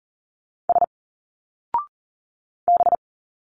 Morse code, heard as a keyed tone, about 40 wpm, 700 Hz. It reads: S B
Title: Single Letters 40wpm